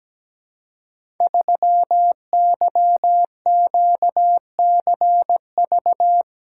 Morse code, heard as a keyed tone, about 17 wpm, 700 Hz. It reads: 3YQCV
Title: Code Groups - 5 Letter:Numbers - 17wpm